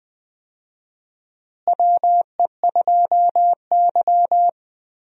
Morse code, heard as a keyed tone, about 20 wpm, 700 Hz. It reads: WE2Y
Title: USA Call Signs - 4 Characters - Repeated 2x - 20wpm